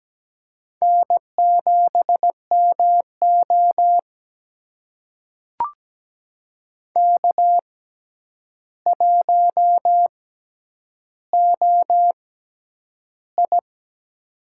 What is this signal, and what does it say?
Morse code, heard as a keyed tone, about 17 wpm, spaced as for 9 wpm, 700 Hz. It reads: N7MO K1OI
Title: USA Call Signs - 4 Characters - 6x Character Spacing - 17wpm